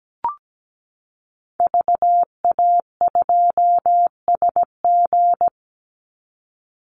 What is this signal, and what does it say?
Morse code, heard as a keyed tone, about 17 wpm, 700 Hz. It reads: VA2SG
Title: Call Signs Repeated 2x - 17wpm